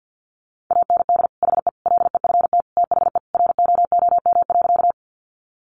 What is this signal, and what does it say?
Morse code, heard as a keyed tone, about 50 wpm, 700 Hz. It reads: AND HE LEFT THE ROOM.
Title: Sentences from Top 300 Words 50wpm